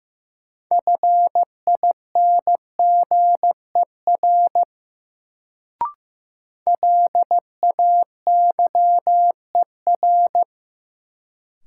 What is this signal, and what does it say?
Morse code, heard as a keyed tone, about 15 wpm, 700 Hz. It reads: FINGER LAYER
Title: Sentences from Top 1600 Words - New Word Drill - 15wpm